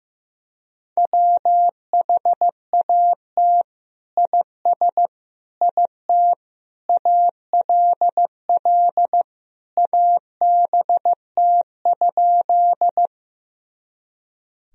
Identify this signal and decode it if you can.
Morse code, heard as a keyed tone, about 15 wpm, 700 Hz. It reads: WHAT IS IT ALL ABT?
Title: Sentences from Top 100 Words and Abbreviations 15wpm